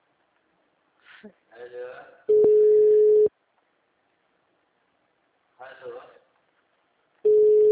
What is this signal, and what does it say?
Ringback tone (Europe, 425 Hz), with speech.
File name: out- -2010- - - .69